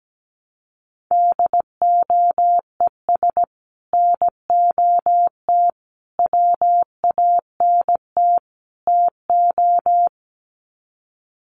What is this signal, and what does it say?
Morse code, heard as a keyed tone, about 17 wpm, 700 Hz. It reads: DOES NOT WANT TO
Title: Sets of 4 Words 17wpm